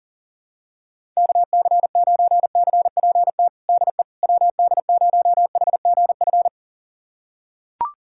Morse code, heard as a keyed tone, about 40 wpm, 700 Hz. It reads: KC9CPT DE WD0HGF
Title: USA Callsign DE Callsign - 6 Characters - 40wpm